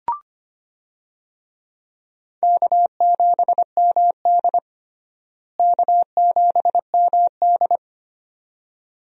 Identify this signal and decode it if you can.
Morse code, heard as a keyed tone, about 25 wpm, 700 Hz. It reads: K7MD K7MD
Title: USA Call Signs - 4 Characters - Repeated 2x - 25wpm